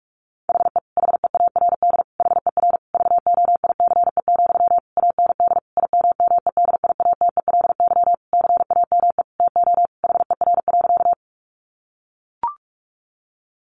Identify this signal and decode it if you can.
Morse code, heard as a keyed tone, about 44 wpm, 700 Hz. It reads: HE HEARD HER VOICE, AND IMMEDIATELY CAME TO HER.
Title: Sentences from Top 1100 Words 45wpm